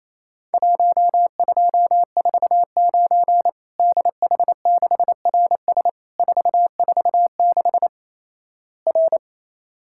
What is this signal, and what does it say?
Morse code, heard as a keyed tone, about 28 wpm, 700 Hz. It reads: 1249 DH6RS 446 R
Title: Worked All Europe DX Contest - QTC Exchange - 28wpm